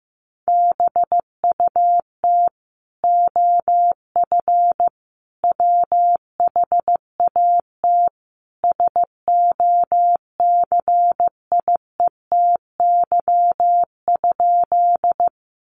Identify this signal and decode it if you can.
Morse code, heard as a keyed tone, about 15 wpm, 700 Hz. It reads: BUT OF WHAT SOCIETY?
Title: Sentences from Top 700 Words 15wpm